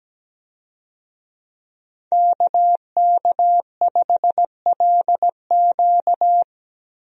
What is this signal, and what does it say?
Morse code, heard as a keyed tone, about 17 wpm, 700 Hz. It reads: KK5LQ
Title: USA Call Signs - 5 Characters - Repeated 2x - 17wpm